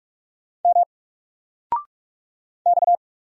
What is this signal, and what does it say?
Morse code, heard as a keyed tone, about 44 wpm, 700 Hz. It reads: M X
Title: Single Character (Letter:Number:Punctuation:Prosign) Alt 45wpm